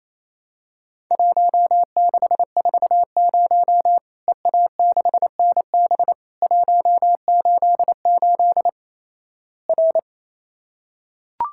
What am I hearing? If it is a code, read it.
Morse code, heard as a keyed tone, about 28 wpm, 700 Hz. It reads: 1640 EA6NB 188 R
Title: Worked All Europe DX Contest - QTC Exchange - 28wpm